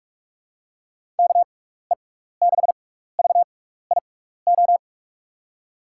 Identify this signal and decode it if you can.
Morse code, heard as a keyed tone, about 47 wpm, spaced as for 16 wpm, 700 Hz. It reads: KE6VIO
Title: USA Call Signs - 6 Characters - 6x Character Spacing - 45wpm